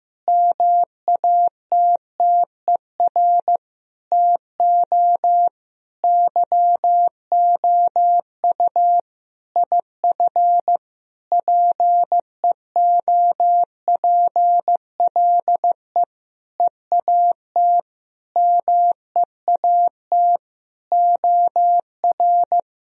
Morse code, heard as a keyed tone, about 15 wpm, 700 Hz. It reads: MATTER TO YOU IF PEOPLE EAT MEAT OR
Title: Sentences from Top 1000 Words - Review Part 3 - 15wpm